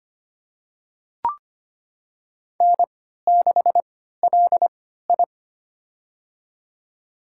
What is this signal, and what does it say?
Morse code, heard as a keyed tone, about 25 wpm, spaced as for 14 wpm, 700 Hz. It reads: N6LI
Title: USA Call Signs - 4 Characters - 3x Character Spacing - 25wpm